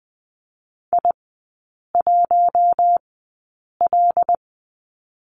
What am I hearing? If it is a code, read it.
Morse code, heard as a keyed tone, about 20 wpm, spaced as for 8 wpm, 700 Hz. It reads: I1L
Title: 114 Lesson 19e - Letter w - Three Characters 20wpm